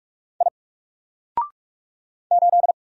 Morse code, heard as a keyed tone, about 44 wpm, 700 Hz. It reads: I 8
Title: Single Letter-Number Alt 45wpm